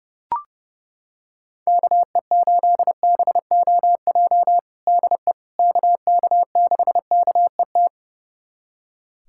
Morse code, heard as a keyed tone, about 30 wpm, 700 Hz. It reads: KE8BOJ DE KK6KET
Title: USA Callsign DE Callsign - 6 Characters - 30wpm